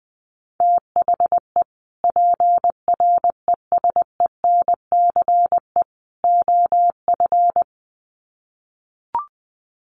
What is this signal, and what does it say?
Morse code, heard as a keyed tone, about 19 wpm, 700 Hz. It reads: THE PRESENCE OF
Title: Sets of 3 Words 20wpm